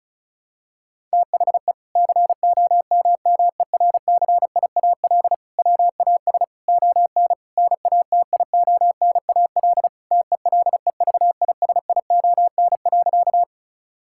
Morse code, heard as a keyed tone, about 36 wpm, 700 Hz. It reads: THE COMMERCIAL WAS ON NATIONAL TELEVISION.
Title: Sentences from Top 1000 Words - Review Part 5 - 35wpm